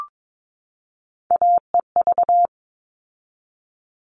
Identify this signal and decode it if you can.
Morse code, heard as a keyed tone, about 22 wpm, 700 Hz. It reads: AEV
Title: All Cut Numbers - 3-Digit 22wpm